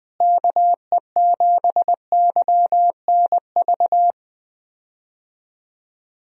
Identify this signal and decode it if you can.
Morse code, heard as a keyed tone, about 20 wpm, 700 Hz. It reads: KE7YNV
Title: USA Call Signs - 6 Characters - Repeated 3x - 20wpm